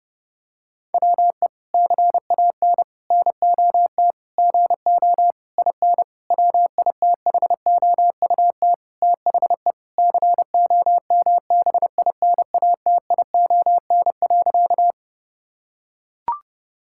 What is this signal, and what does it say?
Morse code, heard as a keyed tone, about 32 wpm, 700 Hz. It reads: WE CAN NOT GO IN WITHOUT THE COMBINATION.
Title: Sentences from Top 1000 Words - Review Part 6 - 30wpm